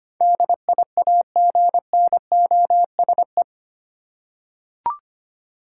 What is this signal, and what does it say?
Morse code, heard as a keyed tone, about 25 wpm, 700 Hz. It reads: DIAGNOSE
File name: Eight Letter Words - 25wpm